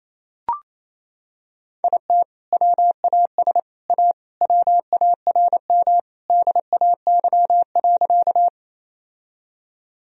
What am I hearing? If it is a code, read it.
Morse code, heard as a keyed tone, about 28 wpm, 700 Hz. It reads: IT WAS A WARM DAY.
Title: Sentences from Top 1000 Words - Review Part 2 - 28wpm